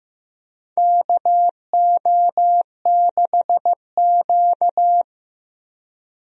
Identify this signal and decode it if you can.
Morse code, heard as a keyed tone, about 15 wpm, 700 Hz. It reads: KO6Q
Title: USA Call Signs - 4 Characters - Repeated 2x - 15wpm